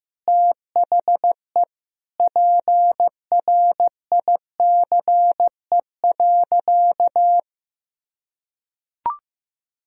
Morse code, heard as a keyed tone, about 16 wpm, 700 Hz. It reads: THE PRICE.
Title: Sentences from Top 1000 Words - Review Part 4 - 15wpm